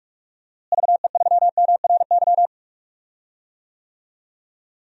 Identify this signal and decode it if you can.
Morse code, heard as a keyed tone, about 46 wpm, 700 Hz. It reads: VE3MRY
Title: DXCC-100 Call Signs - 6 Characters - Repeated 3x - 45wpm